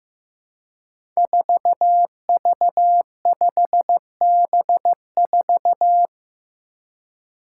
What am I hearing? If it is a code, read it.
Morse code, heard as a keyed tone, about 15 wpm, 700 Hz. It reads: 4V5B4
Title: Five Challenging Letter-Number Combinations 15wpm